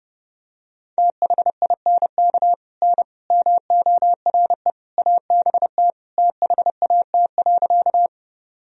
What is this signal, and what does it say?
Morse code, heard as a keyed tone, about 30 wpm, 700 Hz. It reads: THINK N MORE ABT THAT.
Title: Sentences from Top 200 Words and Abbreviations 30wpm